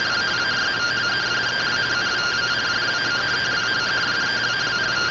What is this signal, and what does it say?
Two-tone teleprinter signal, about 170 Hz shift.
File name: findings / SAGITAL fr15 nb203 3mm.gen-43 rtty rx